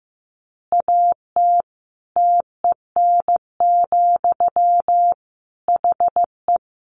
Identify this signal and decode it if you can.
Morse code, heard as a keyed tone, about 15 wpm, 700 Hz. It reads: AT TEN, HE
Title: Easy Phrases 15wpm